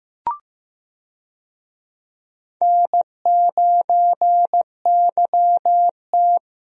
Morse code, heard as a keyed tone, about 15 wpm, 700 Hz. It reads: N9YT
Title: USA Call Signs - 4 Characters - Repeated 2x - 15wpm